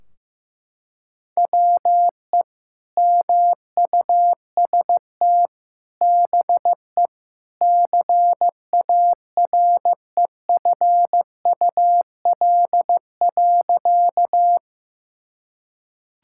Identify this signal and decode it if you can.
Morse code, heard as a keyed tone, about 16 wpm, 700 Hz. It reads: WE MUST BE CAREFUL.
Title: Sentences from Top 1600 Words 15wpm